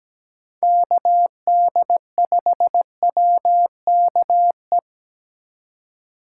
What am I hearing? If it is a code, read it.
Morse code, heard as a keyed tone, about 17 wpm, 700 Hz. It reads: KD5WKE
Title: USA Call Signs - 6 Characters - 2x Character Spacing - 17wpm